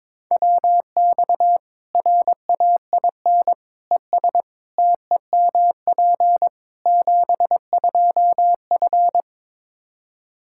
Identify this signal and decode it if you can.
Morse code, heard as a keyed tone, about 22 wpm, 700 Hz. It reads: WX RAIN ES TEMP 72F
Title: K1USN SST QSO Practice 22wpm